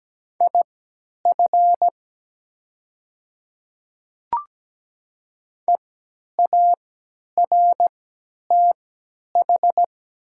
Morse code, heard as a keyed tone, about 17 wpm, spaced as for 10 wpm, 700 Hz. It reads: IF EARTH
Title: 3x Character Spacing - Top 200 Words - 17wpm